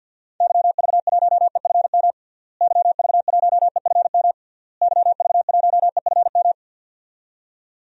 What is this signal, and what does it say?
Morse code, heard as a keyed tone, about 52 wpm, 700 Hz. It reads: YV1EFM YV1EFM YV1EFM
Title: DXCC-100 Call Signs - 6 Characters - Repeated 3x - 50wpm